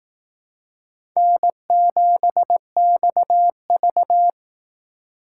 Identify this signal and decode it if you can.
Morse code, heard as a keyed tone, about 19 wpm, 700 Hz. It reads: N7XV
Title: Speed Racing - USA Call Signs 15wpm